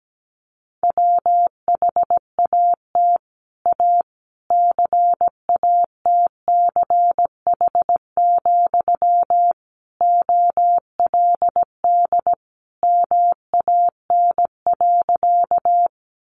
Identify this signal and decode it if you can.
Morse code, heard as a keyed tone, about 17 wpm, 700 Hz. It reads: WHAT A CATCH, OLD MAN.
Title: Sentences from Top 1500 Words 17wpm